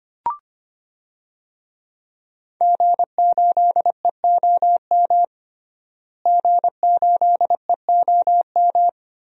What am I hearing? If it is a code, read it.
Morse code, heard as a keyed tone, about 25 wpm, 700 Hz. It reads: G8EOM G8EOM
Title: DXCC-100 Call Signs - 5 Characters - Repeated 2x - 25wpm